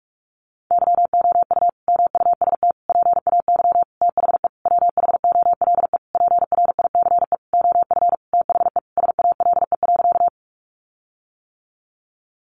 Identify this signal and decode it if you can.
Morse code, heard as a keyed tone, about 44 wpm, 700 Hz. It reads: YOU MUST PAY THE WHOLE PRICE OF THE SALE.
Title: Sentences from Top 1400 Words 45wpm